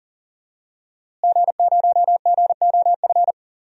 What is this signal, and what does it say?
Morse code, heard as a keyed tone, about 40 wpm, 700 Hz. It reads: G0GOF
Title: DXCC-100 Call Signs - 5 Characters - Repeated 2x - 40wpm